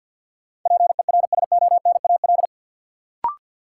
Morse code, heard as a keyed tone, about 48 wpm, 700 Hz. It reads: PERSONAL
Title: Sentences from Top 700 Words - New Word Drill - 50wpm